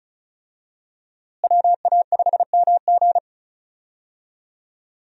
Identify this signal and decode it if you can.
Morse code, heard as a keyed tone, about 36 wpm, 700 Hz. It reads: WA5MG
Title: USA Call Signs - 5 Characters - Repeated 2x - 35wpm